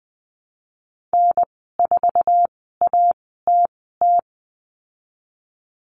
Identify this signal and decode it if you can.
Morse code, heard as a keyed tone, about 20 wpm, spaced as for 14 wpm, 700 Hz. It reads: N4ATT
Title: 069 Lesson 12g - Letter d - Call Signs 20wpm